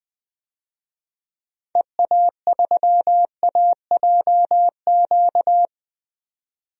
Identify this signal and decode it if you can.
Morse code, heard as a keyed tone, about 20 wpm, 700 Hz. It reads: EA3AJQ
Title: DXCC-100 Call Signs - 6 Characters - 20wpm